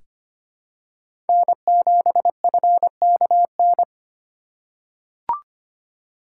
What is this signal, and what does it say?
Morse code, heard as a keyed tone, about 25 wpm, 700 Hz. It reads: N7FKN
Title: USA Call Signs - 5 Characters - 2x Character Spacing - 25wpm